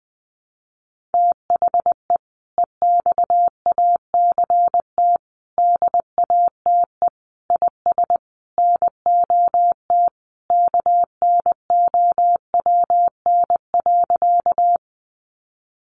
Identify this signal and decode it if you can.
Morse code, heard as a keyed tone, about 20 wpm, 700 Hz. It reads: THE EXACT DATE IS NOT KNOWN.
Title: Sentences from Top 1300 Words 20wpm